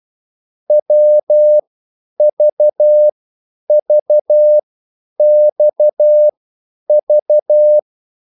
Morse code, heard as a keyed tone, about 12 wpm, spaced as for 9 wpm, 600 Hz. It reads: WVVXV
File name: cwbook